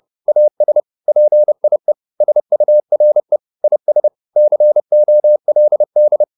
Morse code, heard as a keyed tone, about 30 wpm, 600 Hz. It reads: AS PIE SURE IS COLD